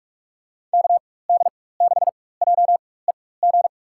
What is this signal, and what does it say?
Morse code, heard as a keyed tone, about 46 wpm, spaced as for 21 wpm, 700 Hz. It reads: KD6JEG
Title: USA Call Signs - 6 Characters - 4x Character Spacing - 45wpm